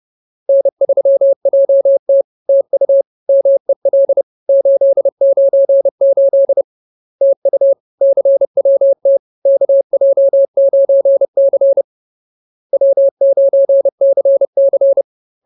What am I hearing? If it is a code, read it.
Morse code, heard as a keyed tone, about 30 wpm, 550 Hz. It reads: N3JT TU MEL 898 TU CWT KJ9C W9CC